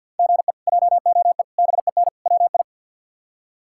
Morse code, heard as a keyed tone, about 49 wpm, 700 Hz. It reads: GE JOE BEN WI
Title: K1USN SST QSO Practice - 50wpm